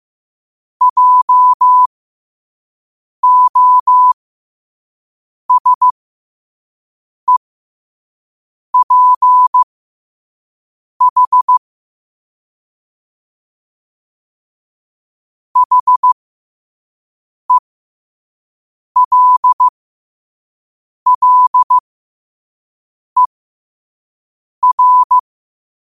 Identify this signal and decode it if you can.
Morse code, heard as a keyed tone, about 15 wpm, spaced as for 5 wpm, 1000 Hz. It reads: JOSEPH HELLER
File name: qotdfiles / 5wpm